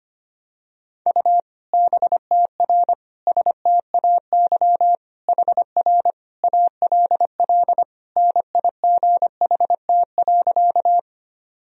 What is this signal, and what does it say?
Morse code, heard as a keyed tone, about 25 wpm, 700 Hz. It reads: U BTR STAY HR ALL NIGHT.
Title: Sentences from Top 500 Words and Abbreviations 25wpm